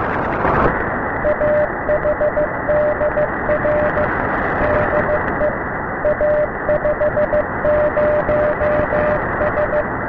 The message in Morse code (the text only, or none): AHDR DE A50S